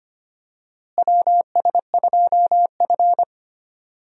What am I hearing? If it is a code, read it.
Morse code, heard as a keyed tone, about 25 wpm, 700 Hz. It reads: WS2F
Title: USA Call Signs - 4 Characters - Repeated 3x - 25wpm